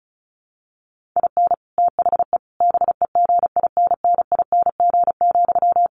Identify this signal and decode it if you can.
Morse code, heard as a keyed tone, about 34 wpm, 700 Hz. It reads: IN THE BEGINNING,
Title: Sentences from Top 1400 Words 35wpm